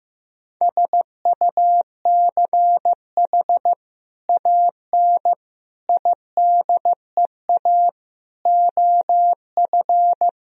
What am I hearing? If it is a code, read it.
Morse code, heard as a keyed tone, about 16 wpm, 700 Hz. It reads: SUCH AN IDEA OF
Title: Sentences from Top 1000 Words - Review Part 3 - 15wpm